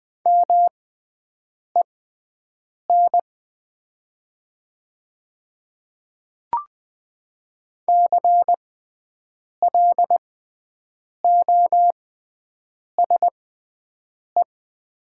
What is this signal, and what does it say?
Morse code, heard as a keyed tone, about 20 wpm, spaced as for 7 wpm, 700 Hz. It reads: MEN CLOSE
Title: 6x Character Spacing - Top 300 Words - 20wpm